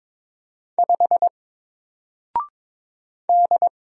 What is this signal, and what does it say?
Morse code, heard as a keyed tone, about 22 wpm, 700 Hz. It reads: 5 D
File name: Single-Letter-Number-22wpm